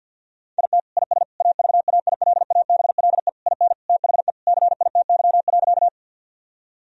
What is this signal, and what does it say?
Morse code, heard as a keyed tone, about 50 wpm, 700 Hz. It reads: IT IS AVAILABLE IN THE CITY.
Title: Sentences from Top 1000 Words - Review Part 3 - 50wpm